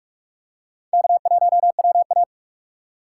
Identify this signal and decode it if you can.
Morse code, heard as a keyed tone, about 48 wpm, 700 Hz. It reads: K1WA
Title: USA Call Signs - 4 Characters - 3x Character Spacing - 45wpm